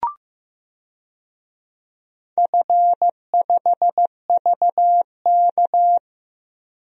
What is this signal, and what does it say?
Morse code, heard as a keyed tone, about 15 wpm, 700 Hz. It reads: F5VK
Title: DXCC-100 Call Signs - 4 Characters - 15wpm